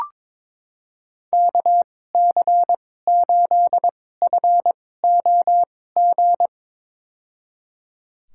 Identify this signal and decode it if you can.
Morse code, heard as a keyed tone, about 22 wpm, spaced as for 16 wpm, 700 Hz. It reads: KC8FOG
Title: USA Call Signs - 6 Characters - 2x Character Spacing - 22wpm